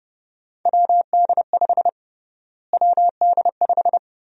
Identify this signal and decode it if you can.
Morse code, heard as a keyed tone, about 30 wpm, 700 Hz. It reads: WD5 WD5
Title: USA Call Sign Prefixes - Repeated 2x - 30wpm